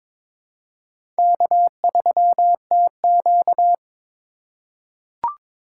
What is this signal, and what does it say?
Morse code, heard as a keyed tone, about 22 wpm, 700 Hz. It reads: K3TQ
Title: USA Call Signs - 4 Characters - 4x Character Spacing - 22wpm